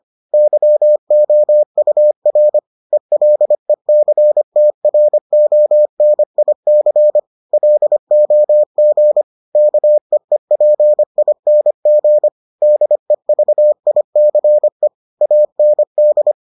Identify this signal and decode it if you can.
Morse code, heard as a keyed tone, about 25 wpm, 600 Hz. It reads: YOUR ELECTRONIC LOG KEEPING DEVICE AND